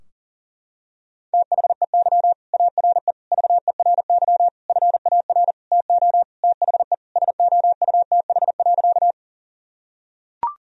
Morse code, heard as a keyed tone, about 40 wpm, 700 Hz. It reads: THEY ARE VERY FAR TO THE SOUTH.
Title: Sentences from Top 1000 Words - Review Part 2 - 40wpm